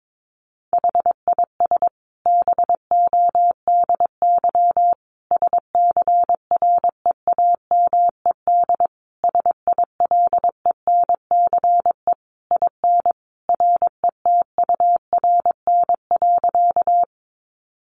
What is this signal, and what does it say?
Morse code, heard as a keyed tone, about 22 wpm, 700 Hz. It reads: HIS BODY SCREAMED SILENCE IN RETURN.
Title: A Sound Of Thunder 22wpm